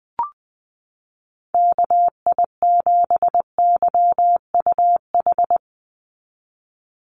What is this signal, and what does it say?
Morse code, heard as a keyed tone, about 20 wpm, 700 Hz. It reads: KI7YUH
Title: USA Call Signs 20wpm